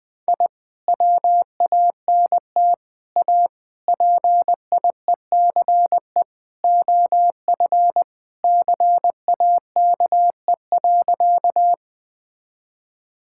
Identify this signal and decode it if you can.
Morse code, heard as a keyed tone, about 20 wpm, 700 Hz. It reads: I WANT A PIECE OF CAKE.
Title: Sentences with 3-Word Phrases - Spoken Beginning - 20wpm